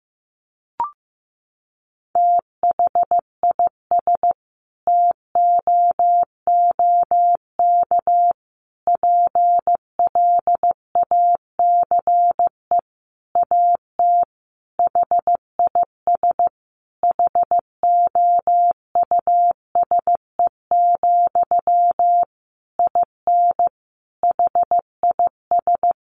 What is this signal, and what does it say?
Morse code, heard as a keyed tone, about 15 wpm, 700 Hz. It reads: THIS TOOK PLACE AT HIS HOUSE, IN HIS
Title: Sentences from Top 300 Words 15wpm